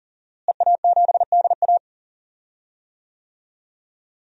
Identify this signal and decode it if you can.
Morse code, heard as a keyed tone, about 40 wpm, 700 Hz. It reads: EA7DA
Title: DXCC-100 Call Signs - 5 Characters - Repeated 3x - 40wpm